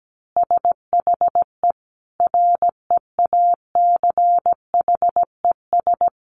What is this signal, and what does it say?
Morse code, heard as a keyed tone, about 17 wpm, 700 Hz. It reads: SHE REACHES
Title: Easy Phrases 17wpm